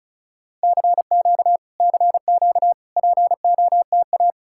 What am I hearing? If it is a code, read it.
Morse code, heard as a keyed tone, about 48 wpm, spaced as for 42 wpm, 700 Hz. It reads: CQ CQ POTA
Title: Speed-Racing Common QSO Phrases Encore - 35wpm